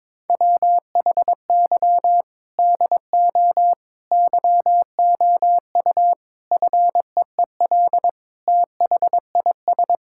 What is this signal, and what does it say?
Morse code, heard as a keyed tone, about 22 wpm, 700 Hz. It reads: WHY DO YOU FEEL THIS
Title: Sentences from Top 1600 Words 22wpm